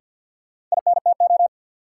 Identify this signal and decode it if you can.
Morse code, heard as a keyed tone, about 47 wpm, 700 Hz. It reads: INTO
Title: Sentences from Top 600 Words - New Word Drill - 50wpm